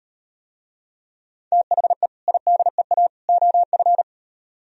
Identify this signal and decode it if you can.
Morse code, heard as a keyed tone, about 38 wpm, 700 Hz. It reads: THE IDEA OF